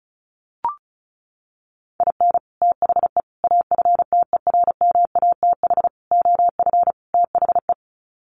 Morse code, heard as a keyed tone, about 34 wpm, 700 Hz. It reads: IN THE AFTERMATH OF THE
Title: Sets of 5 Words 35wpm